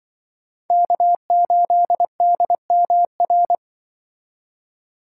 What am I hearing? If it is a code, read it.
Morse code, heard as a keyed tone, about 25 wpm, 700 Hz. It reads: K8DMR
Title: Speed Racing - USA Call Signs 20wpm